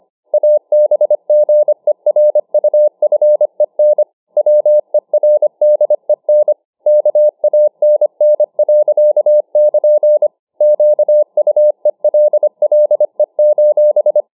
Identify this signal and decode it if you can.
Morse code, heard as a keyed tone, about 25 wpm, 600 Hz. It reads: ABGERUFEN WERDEN KANN. <KN>QUELLE: